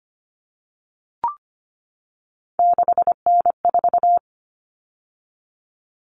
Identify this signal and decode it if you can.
Morse code, heard as a keyed tone, about 25 wpm, 700 Hz. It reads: 6N4
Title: Common Cut Numbers - 3-Digit 25wpm